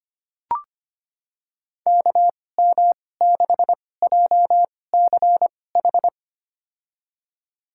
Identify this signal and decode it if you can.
Morse code, heard as a keyed tone, about 25 wpm, spaced as for 18 wpm, 700 Hz. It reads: KM6JCH
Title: USA Call Signs - 6 Characters - 2x Character Spacing - 25wpm